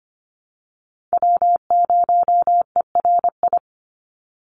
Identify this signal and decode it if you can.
Morse code, heard as a keyed tone, about 25 wpm, 700 Hz. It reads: W0ERI
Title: NA SOTA Chaser Callsigns 25wpm